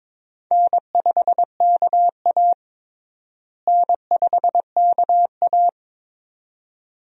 Morse code, heard as a keyed tone, about 22 wpm, 700 Hz. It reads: N5KA N5KA
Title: USA Call Signs - 4 Characters - Repeated 2x - 22wpm